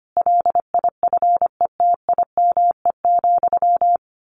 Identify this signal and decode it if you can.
Morse code, heard as a keyed tone, about 25 wpm, 700 Hz. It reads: LIFETIME,
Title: A Sound Of Thunder 25wpm